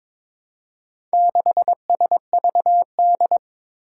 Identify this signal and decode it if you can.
Morse code, heard as a keyed tone, about 22 wpm, 700 Hz. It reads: 6SVD
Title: Four Challenging Letter-Number Combinations 22wpm